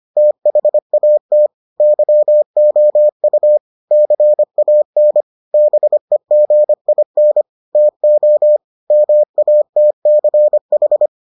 Morse code, heard as a keyed tone, about 25 wpm, 600 Hz. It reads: THAT YOU CAN BEGIN TO MATCH